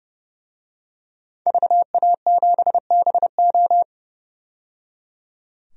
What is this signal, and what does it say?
Morse code, heard as a keyed tone, about 29 wpm, 690 Hz. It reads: VA7BO
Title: DXCC-100 Call Signs - 5 Characters - 30wpm